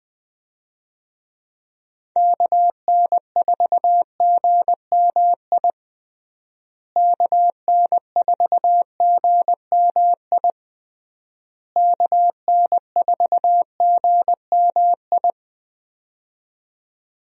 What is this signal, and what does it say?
Morse code, heard as a keyed tone, about 20 wpm, 700 Hz. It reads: KN4GMI KN4GMI KN4GMI
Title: USA Call Signs - 6 Characters - Repeated 3x - 20wpm